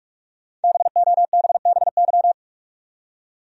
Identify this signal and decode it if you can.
Morse code, heard as a keyed tone, about 46 wpm, 700 Hz. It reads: BOBBY
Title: USA Common Names - Max 5 Letters - 45wpm